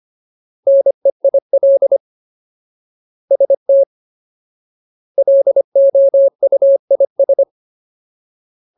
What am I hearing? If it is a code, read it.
Morse code, heard as a keyed tone, about 25 wpm, 550 Hz. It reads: NEIL ST LOUIS
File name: Ultimate Word List / 25WPM Ultimate